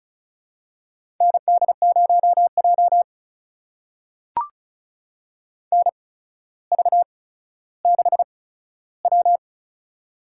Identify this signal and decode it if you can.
Morse code, heard as a keyed tone, about 36 wpm, spaced as for 15 wpm, 700 Hz. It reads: ND0J NV6W
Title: USA Call Signs - 4 Characters - 8x Character Spacing - 35wpm